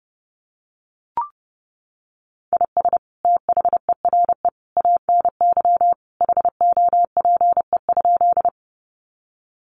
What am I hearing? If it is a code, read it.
Morse code, heard as a keyed tone, about 30 wpm, 700 Hz. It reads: IS THERE ANY HOPE?
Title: Sentences from Top 1000 Words 30wpm